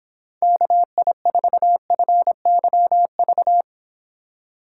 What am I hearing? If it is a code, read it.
Morse code, heard as a keyed tone, about 26 wpm, 700 Hz. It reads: KI4FYV